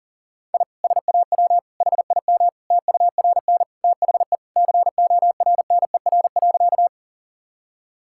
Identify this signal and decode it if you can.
Morse code, heard as a keyed tone, about 37 wpm, 700 Hz. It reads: I SAW HIM TURN THE CORNER.
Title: Sentences from Top 1600 Words 40wpm